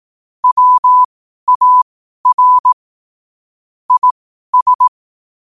Morse code, heard as a keyed tone, about 18 wpm, spaced as for 12 wpm, 1000 Hz. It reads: WAR IS
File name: qotdfiles / 12wpm